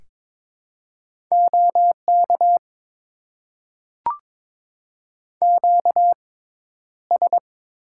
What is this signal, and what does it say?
Morse code, heard as a keyed tone, about 22 wpm, spaced as for 11 wpm, 700 Hz. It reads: OK QS
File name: Top 90 QSO Elements - 6x Character Spacing - 22wpm